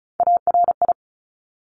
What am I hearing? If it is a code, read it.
Morse code, heard as a keyed tone, about 35 wpm, 700 Hz. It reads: ARI
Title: Top Three Letter Combinations in Words 35wpm